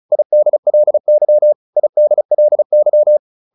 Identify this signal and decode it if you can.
Morse code, heard as a keyed tone, about 34 wpm, 600 Hz. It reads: IDLY IDLY